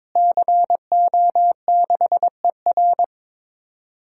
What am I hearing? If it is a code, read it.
Morse code, heard as a keyed tone, about 22 wpm, 700 Hz. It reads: CO6ER
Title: DXCC-100 Call Signs - 5 Characters - Repeated 3x - 22wpm